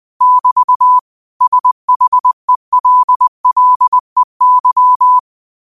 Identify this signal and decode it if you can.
Morse code, heard as a keyed tone, about 20 wpm, 1000 Hz. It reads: <BT> SHELLEY